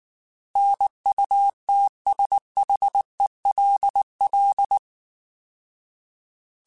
Morse code, heard as a keyed tone, about 19 wpm, 800 Hz. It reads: NUTSHELL
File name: Alphabetical / Alphabetical mp3s / Ns @18